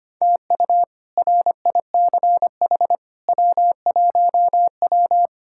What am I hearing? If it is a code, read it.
Morse code, heard as a keyed tone, about 25 wpm, 700 Hz. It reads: TU RICH W1W
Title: K1USN SST QSO Practice - 25wpm